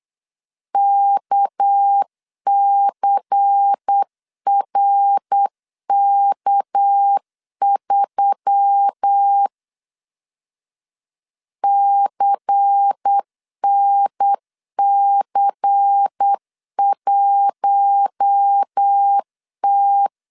Morse code, heard as a keyed tone, about 8 wpm, 800 Hz. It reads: KCRK3 CNC1T
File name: CW / DET / 1 6 1500C S A 8wpm 5203